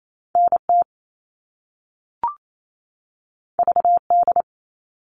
Morse code, heard as a keyed tone, about 28 wpm, 700 Hz. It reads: NT VD
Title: All Cut Numbers - 2-Digit 28wpm